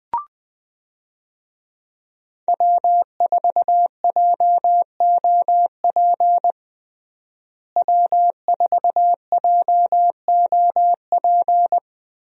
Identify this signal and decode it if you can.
Morse code, heard as a keyed tone, about 20 wpm, 700 Hz. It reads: W4JOP W4JOP
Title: USA Call Signs - 5 Characters - Repeated 3x - 20wpm